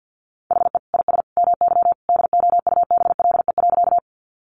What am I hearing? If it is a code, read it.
Morse code, heard as a keyed tone, about 50 wpm, 700 Hz. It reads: HE IS MY DOUBLE.
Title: Sentences from Top 1400 Words 50wpm